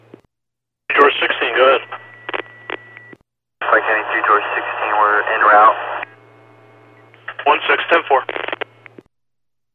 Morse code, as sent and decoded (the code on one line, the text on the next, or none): none